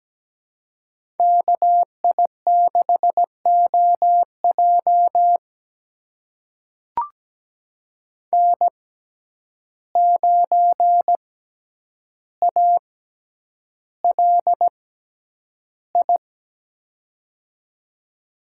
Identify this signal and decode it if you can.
Morse code, heard as a keyed tone, about 17 wpm, spaced as for 9 wpm, 700 Hz. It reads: KI6OJ N9ALI
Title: USA Call Signs - 5 Characters - 6x Character Spacing - 17wpm